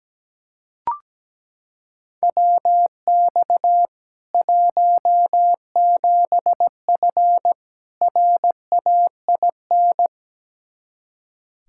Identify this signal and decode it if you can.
Morse code, heard as a keyed tone, about 17 wpm, 700 Hz. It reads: WX 17F RAIN
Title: K1USN SST QSO Practice 17wpm